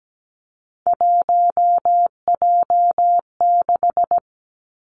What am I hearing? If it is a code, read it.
Morse code, heard as a keyed tone, about 17 wpm, 700 Hz. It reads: 1J6
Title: Three Challenging Letter-Number Combinations 17wpm